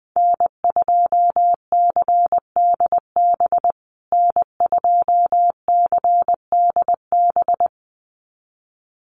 Morse code, heard as a keyed tone, about 20 wpm, 700 Hz. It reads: N2CDB N2CDB
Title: USA Call Signs Repeated 2x 20wpm